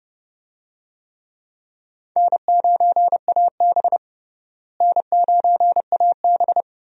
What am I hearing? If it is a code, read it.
Morse code, heard as a keyed tone, about 30 wpm, 700 Hz. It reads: N9AB N9AB
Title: USA Call Signs - 4 Characters - Repeated 2x - 30wpm